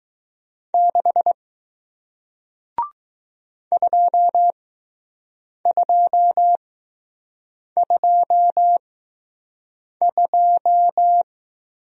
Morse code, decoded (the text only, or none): 6 2222